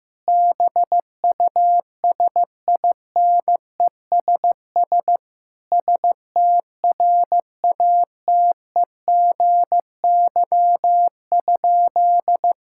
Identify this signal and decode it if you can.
Morse code, heard as a keyed tone, about 15 wpm, 700 Hz. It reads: BUSINESS STRATEGY?
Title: Sentences from Top 1000 Words - Review Part 5 - 15wpm